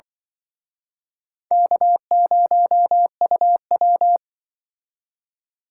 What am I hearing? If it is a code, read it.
Morse code, heard as a keyed tone, about 24 wpm, 700 Hz. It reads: K0UW
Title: Speed Racing - USA 1x2 Call Signs 20wpm